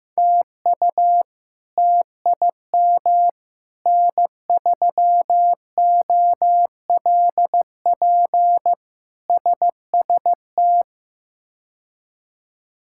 Morse code, heard as a keyed tone, about 16 wpm, 700 Hz. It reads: TU TIM N3OLP SST
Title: K1USN SST QSO Practice - 15wpm